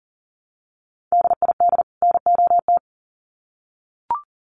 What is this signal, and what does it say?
Morse code, heard as a keyed tone, about 40 wpm, 700 Hz. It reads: DID NOT
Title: Sets of 2 Words 40wpm